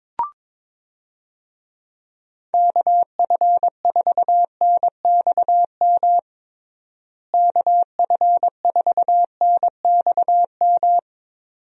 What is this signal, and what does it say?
Morse code, heard as a keyed tone, about 22 wpm, 700 Hz. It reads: KF4NXM KF4NXM
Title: USA Call Signs - 6 Characters - Repeated 2x - 22wpm